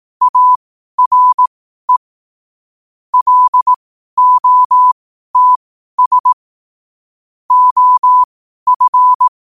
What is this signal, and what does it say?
Morse code, heard as a keyed tone, about 18 wpm, spaced as for 12 wpm, 1000 Hz. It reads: ARE LOTS OF
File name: qotdfiles / 12wpm